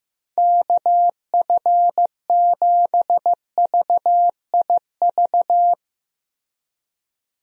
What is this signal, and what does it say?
Morse code, heard as a keyed tone, about 16 wpm, 700 Hz. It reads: KF7VIV
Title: USA Call Signs - 6 Characters - 15wpm